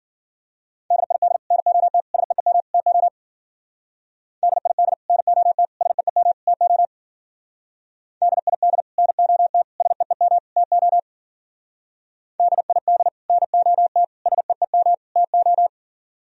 Morse code, heard as a keyed tone, about 47 wpm, 700 Hz. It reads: DID NOT SEEM TO DID NOT SEEM TO DID NOT SEEM TO DID NOT SEEM TO
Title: Speed-Racing - Sets of 4 Words 40wpm